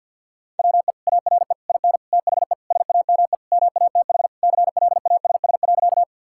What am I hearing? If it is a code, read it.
Morse code, heard as a keyed tone, about 50 wpm, 700 Hz. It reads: WE ARE IN THE SAME MATH CLASS.
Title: Sentences from Top 1000 Words - Review Part 6 - 50wpm